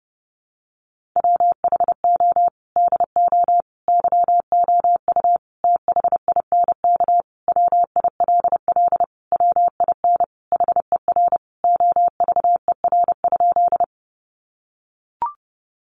Morse code, heard as a keyed tone, about 30 wpm, 700 Hz. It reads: WHO DO YOU THINK WILL WIN HER OVER?
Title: Sentences from Top 1500 Words 30wpm